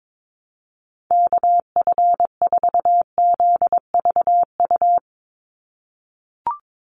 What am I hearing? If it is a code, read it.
Morse code, heard as a keyed tone, about 22 wpm, 700 Hz. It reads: KF4ZVU
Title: USA Call Signs 22wpm